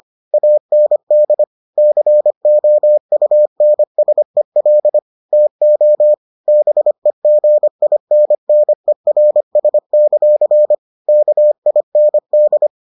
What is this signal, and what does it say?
Morse code, heard as a keyed tone, about 25 wpm, 600 Hz. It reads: AND COUNSEL TO BEGINNERS; KIND